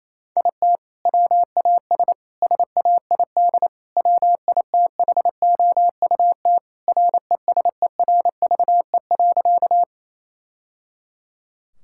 Morse code, heard as a keyed tone, about 28 wpm, 700 Hz. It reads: IT WAS SAID WITHOUT RESERVE.
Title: Sentences from Top 1000 Words - Review Part 6 - 28wpm